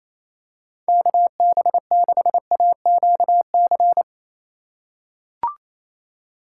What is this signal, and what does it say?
Morse code, heard as a keyed tone, about 28 wpm, 700 Hz. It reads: KB6AQC
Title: USA Call Signs - 2x Character Spacing - 28wpm